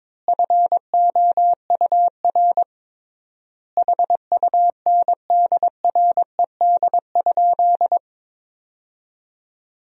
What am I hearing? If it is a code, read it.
Morse code, heard as a keyed tone, about 22 wpm, 700 Hz. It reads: FOUR HUNDRED?
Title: 3x Word-Spacing - Sentences from Top 400 Words - 22wpm